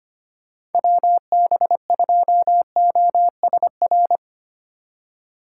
Morse code, heard as a keyed tone, about 32 wpm, 700 Hz. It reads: WB2OSR